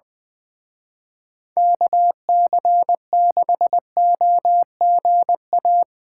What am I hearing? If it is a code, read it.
Morse code, heard as a keyed tone, about 22 wpm, 700 Hz. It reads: KC6OGA
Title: Speed Racing - USA 2x3 Call Signs 20wpm